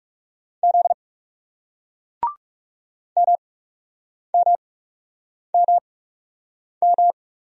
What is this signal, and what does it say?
Morse code, heard as a keyed tone, about 40 wpm, spaced as for 8 wpm, 700 Hz. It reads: Z MMMM